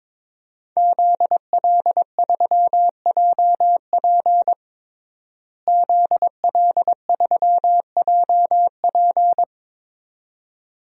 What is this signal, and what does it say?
Morse code, heard as a keyed tone, about 22 wpm, 700 Hz. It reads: ZL3JP ZL3JP
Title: DXCC-100 Call Signs - 5 Characters - Repeated 2x - 22wpm